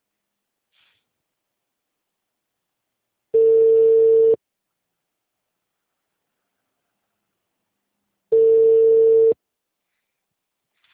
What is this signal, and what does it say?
Ringback tone (China, 450 Hz).